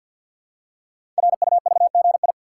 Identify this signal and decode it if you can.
Morse code, heard as a keyed tone, about 51 wpm, 700 Hz. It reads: RUVGI
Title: Code Groups - 5 Letter:Numbers - 50wpm